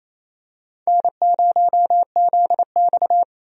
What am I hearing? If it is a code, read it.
Morse code, heard as a keyed tone, about 28 wpm, 700 Hz. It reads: N0ZX
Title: USA Call Signs - 4 Characters - 28wpm